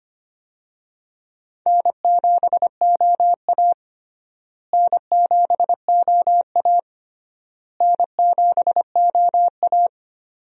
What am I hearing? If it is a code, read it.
Morse code, heard as a keyed tone, about 25 wpm, 700 Hz. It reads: N7OA N7OA N7OA
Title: USA Call Signs - 4 Characters - Repeated 3x - 25wpm